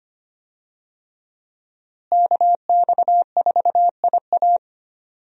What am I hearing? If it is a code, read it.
Morse code, heard as a keyed tone, about 25 wpm, 700 Hz. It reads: KX4IA
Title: USA Call Signs - 5 Characters - Repeated 3x - 25wpm